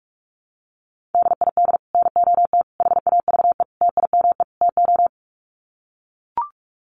Morse code, heard as a keyed tone, about 44 wpm, 700 Hz. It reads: DID NOT HAVE TIME TO
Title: Sets of 5 Words Encore 45wpm